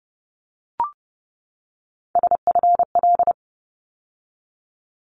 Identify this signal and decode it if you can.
Morse code, heard as a keyed tone, about 30 wpm, 700 Hz. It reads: SFL
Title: ARRL Sections 30wpm